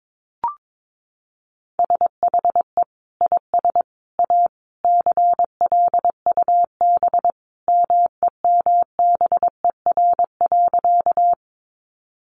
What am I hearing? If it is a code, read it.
Morse code, heard as a keyed tone, about 22 wpm, 700 Hz. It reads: SHE IS A CLUB MEMBER.
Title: Sentences from Top 1200 Words 22wpm